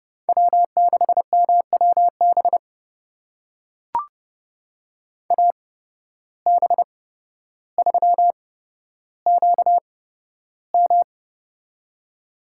Morse code, heard as a keyed tone, about 29 wpm, spaced as for 13 wpm, 700 Hz. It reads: W6MWB AB3QM
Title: USA Call Signs - 5 Characters - 8x Character Spacing - 30wpm